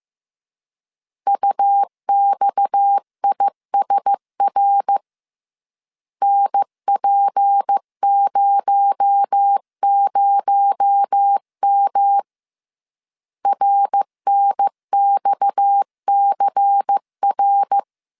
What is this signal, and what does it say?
Morse code, heard as a keyed tone, about 15 wpm, 800 Hz. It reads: UXISR NP00M RNXCR